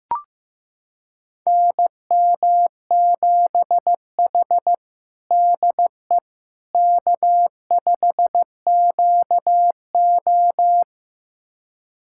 Morse code, heard as a keyed tone, about 16 wpm, 700 Hz. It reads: NM7H DE K5QO
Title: USA Callsign DE Callsign - 4 Characters - 15wpm